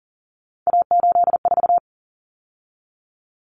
Morse code, heard as a keyed tone, about 40 wpm, 700 Hz. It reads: A84
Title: Common Cut Numbers - 3-Digit 40wpm